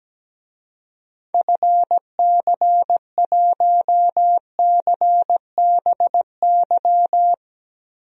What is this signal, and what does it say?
Morse code, heard as a keyed tone, about 17 wpm, 700 Hz. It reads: FC1CBY
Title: DXCC-100 Call Signs - 6 Characters - Repeated 2x - 17wpm